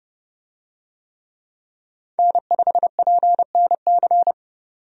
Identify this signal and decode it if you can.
Morse code, heard as a keyed tone, about 30 wpm, 700 Hz. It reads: N5PNC
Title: USA Call Signs - 5 Characters - 30wpm